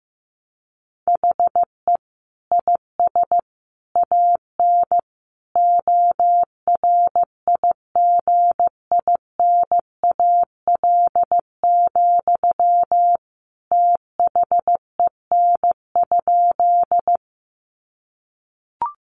Morse code, heard as a keyed tone, about 15 wpm, 700 Hz. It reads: HE IS AN ORIGINAL, THEN?
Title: Sentences from Top 900 Words 15wpm